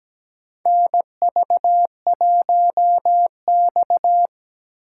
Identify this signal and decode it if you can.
Morse code, heard as a keyed tone, about 17 wpm, 700 Hz. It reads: NV1X
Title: USA Call Signs - 4 Characters - 8x Character Spacing - 17wpm